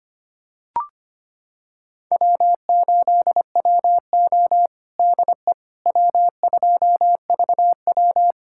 Morse code, heard as a keyed tone, about 25 wpm, 700 Hz. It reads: W8WO DE W2VW
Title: USA Callsign DE Callsign - 4 Characters - 25wpm